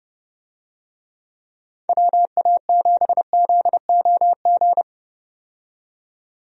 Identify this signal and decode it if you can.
Morse code, heard as a keyed tone, about 30 wpm, 700 Hz. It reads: WA7ZOG
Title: USA Call Signs - 6 Characters - 30wpm